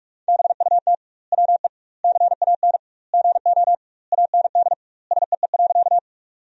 Morse code, heard as a keyed tone, about 46 wpm, 700 Hz. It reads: BUT WE CAN GO AND SEE.
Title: 2x Word-Spacing - Sentences from Top 100 Words - 45wpm